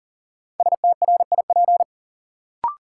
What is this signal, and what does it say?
Morse code, heard as a keyed tone, about 40 wpm, 700 Hz. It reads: STRIP
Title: Sentences from Top 1600 Words - New Word Drill - 40wpm